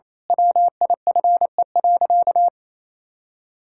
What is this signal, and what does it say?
Morse code, heard as a keyed tone, about 28 wpm, 700 Hz. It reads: WIFE.
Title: Pangram Challenge v2 - 28wpm